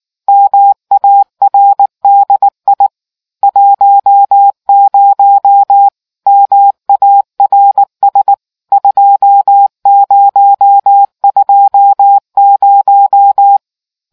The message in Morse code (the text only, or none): MARDI 10 MARS 2020